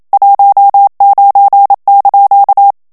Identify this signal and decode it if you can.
Morse code, heard as a keyed tone, about 27 wpm, 800 Hz. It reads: 19)
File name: CW / FICHIERS-MP3 / FRANCAIS TEXTE CLAIR 1 VITESSE 26 MOTS MINUTES